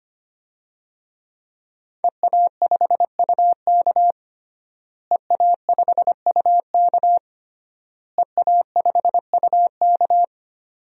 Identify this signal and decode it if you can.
Morse code, heard as a keyed tone, about 25 wpm, 700 Hz. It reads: EA5UK EA5UK EA5UK
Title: DXCC-100 Call Signs - 5 Characters - Repeated 3x - 25wpm